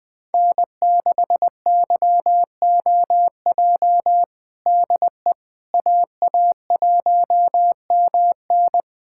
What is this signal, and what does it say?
Morse code, heard as a keyed tone, about 20 wpm, 700 Hz. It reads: N6YOJ DE AA1MN
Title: USA Callsign DE Callsign - 5 Characters - 20wpm